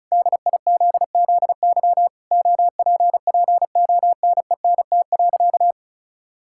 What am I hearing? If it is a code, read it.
Morse code, heard as a keyed tone, about 36 wpm, 700 Hz. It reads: DIZZY OPPONENT.
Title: Pangram Challenge v2 - 35wpm